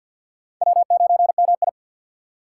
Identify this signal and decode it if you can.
Morse code, heard as a keyed tone, about 49 wpm, 700 Hz. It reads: W9MI
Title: USA Call Signs - 2x Character Spacing - 50wpm